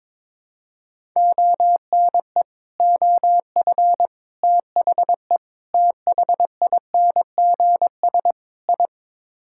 Morse code, heard as a keyed tone, about 22 wpm, 700 Hz. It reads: ONE OF THE THINGS I
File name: practice / 22wpm / Build A Phrase - 5-Word Phrases - 22wpm